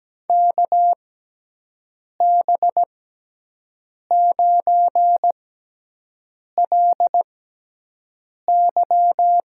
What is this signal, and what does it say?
Morse code, heard as a keyed tone, about 17 wpm, spaced as for 6 wpm, 700 Hz. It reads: KB9LY
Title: USA Call Signs - 6x Character Spacing - 17wpm